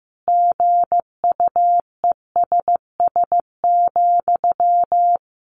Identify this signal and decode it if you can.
Morse code, heard as a keyed tone, about 15 wpm, 700 Hz. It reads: GUESS,
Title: Sentences from Top 1500 Words 15wpm